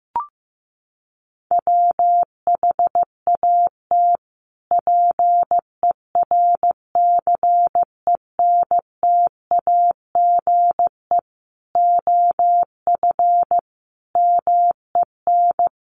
Morse code, heard as a keyed tone, about 15 wpm, 700 Hz. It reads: WHAT PERCENTAGE OF MEN
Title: Sentences from Top 1300 Words 15wpm